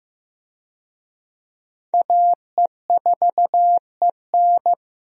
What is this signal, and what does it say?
Morse code, heard as a keyed tone, about 15 wpm, 700 Hz. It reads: AE4EN
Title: USA Call Signs - 5 Characters - Repeated 3x - 15wpm